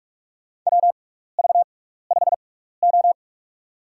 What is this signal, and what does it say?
Morse code, heard as a keyed tone, about 44 wpm, spaced as for 15 wpm, 700 Hz. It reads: WV5O
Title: USA Call Signs - 6x Character Spacing - 45wpm